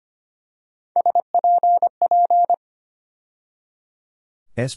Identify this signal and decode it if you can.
Morse code, heard as a keyed tone, about 25 wpm, 700 Hz. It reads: SPP
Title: USA Call Sign Postfixes - 3 Letters 25wpm